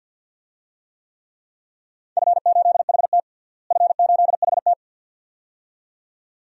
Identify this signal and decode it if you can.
Morse code, heard as a keyed tone, about 48 wpm, 700 Hz. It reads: F8HT F8HT
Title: DXCC-100 Call Signs - 4 Characters - Repeated 2x - 50wpm